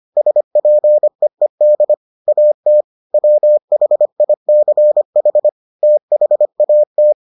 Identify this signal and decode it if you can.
Morse code, heard as a keyed tone, about 25 wpm, 600 Hz. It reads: SPEED AT WHICH THAT